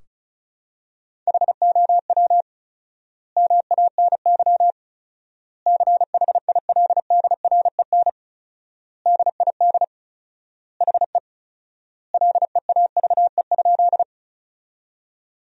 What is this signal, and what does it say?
Morse code, heard as a keyed tone, about 36 wpm, 700 Hz. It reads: HOW MANY CHILDREN DID HE LEAVE?
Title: 4x Word-Spacing - Sentences from Top 400 Words - 35wpm